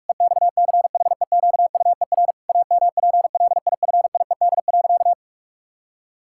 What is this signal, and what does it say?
Morse code, heard as a keyed tone, about 46 wpm, 700 Hz. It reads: EXCHEQUER AMPLIFIED.
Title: Pangram Challenge v2 - 45wpm